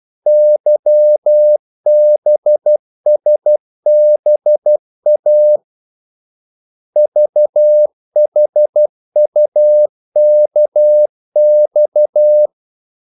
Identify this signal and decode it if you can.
Morse code, heard as a keyed tone, about 12 wpm, 600 Hz. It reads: YBSBA VHUKX
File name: code-exemple